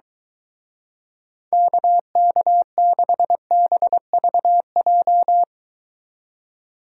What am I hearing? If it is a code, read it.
Morse code, heard as a keyed tone, about 23 wpm, 700 Hz. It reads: KK6BVJ